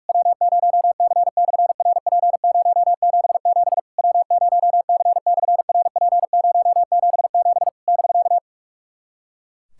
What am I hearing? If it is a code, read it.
Morse code, heard as a keyed tone, about 46 wpm, 700 Hz. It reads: W0C/RP077 W0C/RP077 <BK>
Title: SOTA QSO Practice 45wpm